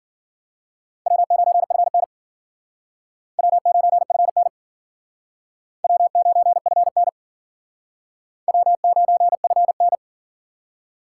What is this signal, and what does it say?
Morse code, heard as a keyed tone, about 52 wpm, 700 Hz. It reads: W9FN W9FN W9FN W9FN